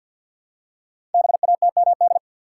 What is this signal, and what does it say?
Morse code, heard as a keyed tone, about 51 wpm, 700 Hz. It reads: BATMD
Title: Code Groups - 5 Letters - 50wpm